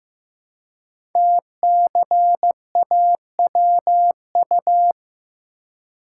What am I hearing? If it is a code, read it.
Morse code, heard as a keyed tone, about 15 wpm, 700 Hz. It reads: TCAWU
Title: Code Groups - 5 Letter:Numbers - 15wpm